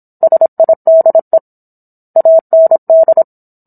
Morse code, heard as a keyed tone, about 27 wpm, 650 Hz. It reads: SIDE AND